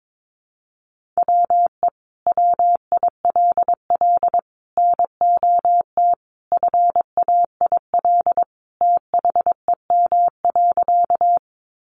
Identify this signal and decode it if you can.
Morse code, heard as a keyed tone, about 22 wpm, 700 Hz. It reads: WE WILL NOT FAIL THEM.
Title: Sentences from Top 1100 Words 22wpm